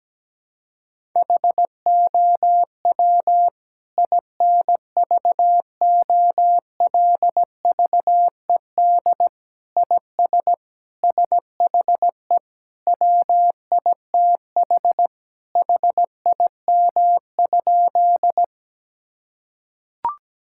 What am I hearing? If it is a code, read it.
Morse code, heard as a keyed tone, about 17 wpm, 700 Hz. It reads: HOW INVOLVED IS SHE WITH HIM?
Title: Sentences from Top 1000 Words - Review Part 5 - 17wpm